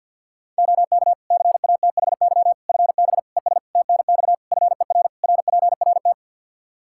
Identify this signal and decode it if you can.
Morse code, heard as a keyed tone, about 52 wpm, 700 Hz. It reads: OK CATHY FB ES TNX FER RPRT
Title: Practice QSO - Exchange 2 - Info - 50wpm